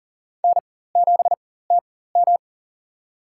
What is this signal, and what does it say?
Morse code, heard as a keyed tone, about 40 wpm, spaced as for 19 wpm, 690 Hz. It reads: N7TM
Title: USA Call Signs - 4x Character Spacing - 40wpm